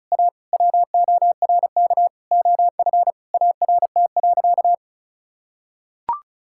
Morse code, heard as a keyed tone, about 34 wpm, 700 Hz. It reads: A WORK OF ART.
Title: Sentences from Top 1000 Words - Review Part 6 - 35wpm